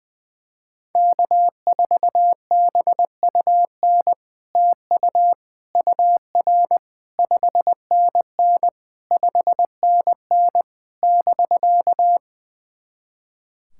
Morse code, heard as a keyed tone, about 20 wpm, 690 Hz. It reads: K4BUN TU UR 5NN 5NN <BK>
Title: SOTA QSO Practice 20wpm